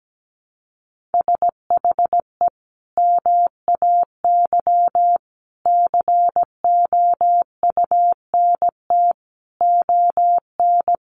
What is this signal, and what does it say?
Morse code, heard as a keyed tone, about 17 wpm, 700 Hz. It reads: SHE MAY COUNT ON
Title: Sentences from Top 1500 Words 17wpm